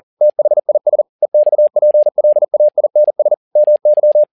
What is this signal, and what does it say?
Morse code, heard as a keyed tone, about 40 wpm, 600 Hz. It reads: THIS EXPLAINS MY
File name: learning center / tower trouble / tower touble 40wpm